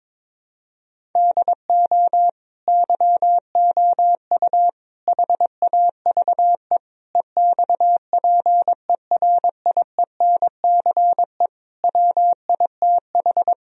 Morse code, heard as a keyed tone, about 21 wpm, 700 Hz. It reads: DO YOU HAVE EXPERIENCE WITH
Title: Sentences from Top 1000 Words - Review Part 3 - 22wpm